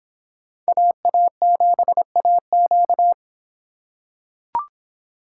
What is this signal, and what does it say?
Morse code, heard as a keyed tone, about 25 wpm, 700 Hz. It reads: AA7AQ